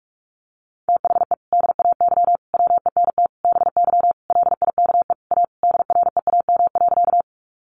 Morse code, heard as a keyed tone, about 44 wpm, 700 Hz. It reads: THE DAY WENT BY LIKE A DREAM.
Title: Sentences from Top 1500 Words 45wpm